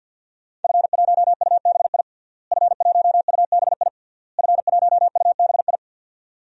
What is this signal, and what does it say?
Morse code, heard as a keyed tone, about 49 wpm, 700 Hz. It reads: F1UBI F1UBI F1UBI
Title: DXCC-100 Call Signs - 5 Characters - Repeated 3x - 50wpm